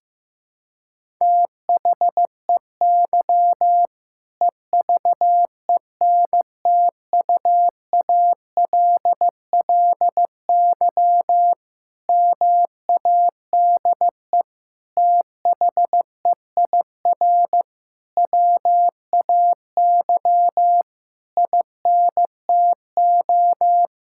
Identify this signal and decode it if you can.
Morse code, heard as a keyed tone, about 15 wpm, 700 Hz. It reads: THEY EVENTUALLY MADE THEIR WAY INTO
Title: Sentences from Top 1000 Words - Review Part 6 - 15wpm